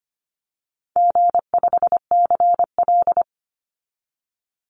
Morse code, heard as a keyed tone, about 25 wpm, 700 Hz. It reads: G5CL
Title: Easy Call Signs - 25wpm